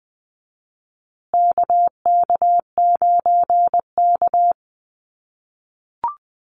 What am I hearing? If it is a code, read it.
Morse code, heard as a keyed tone, about 19 wpm, 700 Hz. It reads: KK9K
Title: USA Call Signs Repeated 3x 20wpm